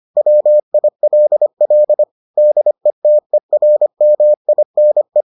Morse code, heard as a keyed tone, about 25 wpm, 600 Hz. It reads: WILL DETERMINE